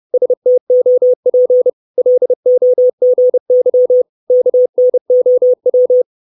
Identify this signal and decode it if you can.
Morse code, heard as a keyed tone, about 30 wpm, 500 Hz. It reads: STOP LOGY KNOW
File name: 632 Words 30WPM